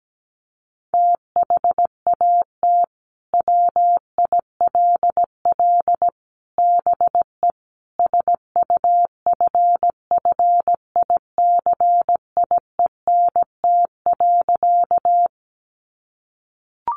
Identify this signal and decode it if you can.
Morse code, heard as a keyed tone, about 17 wpm, 700 Hz. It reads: THAT WILL BE SUFFICIENT.
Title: Sentences from Top 1400 Words 17wpm